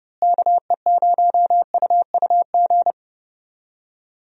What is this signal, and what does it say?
Morse code, heard as a keyed tone, about 32 wpm, 700 Hz. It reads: KE0UUG
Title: USA Call Signs - 6 Characters - Repeated 2x - 30wpm